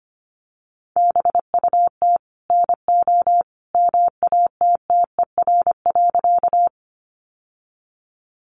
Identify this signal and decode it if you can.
Morse code, heard as a keyed tone, about 25 wpm, 700 Hz. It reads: BUT NO MATTER.
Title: Sentences from Top 700 Words 25wpm